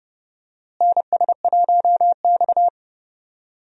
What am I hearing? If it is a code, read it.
Morse code, heard as a keyed tone, about 32 wpm, 700 Hz. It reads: NS1X
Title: USA Call Signs - 4 Characters - Repeated 2x - 30wpm